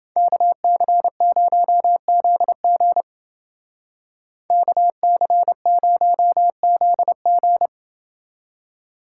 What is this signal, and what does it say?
Morse code, heard as a keyed tone, about 28 wpm, 700 Hz. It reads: KC0ZG KC0ZG